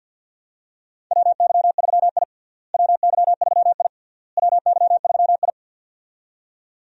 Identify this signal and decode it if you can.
Morse code, heard as a keyed tone, about 50 wpm, 700 Hz. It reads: WY3I WY3I WY3I
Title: USA Call Signs - 4 Characters - Repeated 3x - 50wpm